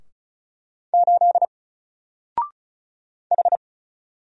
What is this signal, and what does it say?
Morse code, heard as a keyed tone, about 33 wpm, 700 Hz. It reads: 8 H
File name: Single-Letter-Number-35wpm